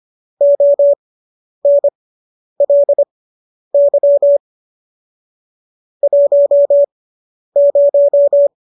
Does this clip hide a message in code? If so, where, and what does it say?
Yes: Morse code from 0.4 to 8.5 s, heard as a keyed tone, about 25 wpm, spaced as for 10 wpm, 570 Hz. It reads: ONLY 10